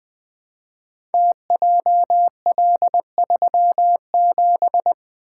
Morse code, heard as a keyed tone, about 20 wpm, 700 Hz. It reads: TJL37
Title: Code Groups - 5 Letter:Numbers - 20wpm